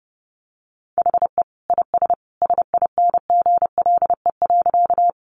Morse code, heard as a keyed tone, about 30 wpm, 700 Hz. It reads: HE IS SINGLE.
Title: Sentences from Top 700 Words 30wpm